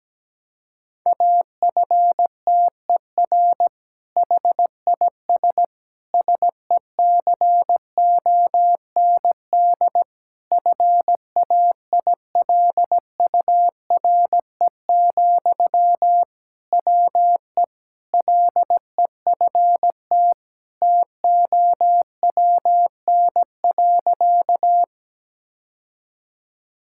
Morse code, heard as a keyed tone, about 17 wpm, 700 Hz. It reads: AFTER HIS SECOND FAILURE, WE LEFT TOWN.
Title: Sentences from Top 1000 Words - Review Part 6 - 17wpm